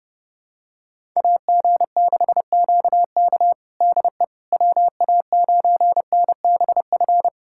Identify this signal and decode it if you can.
Morse code, heard as a keyed tone, about 32 wpm, 700 Hz. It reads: AG6QK DE WA9NBF
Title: USA Callsign DE Callsign 30wpm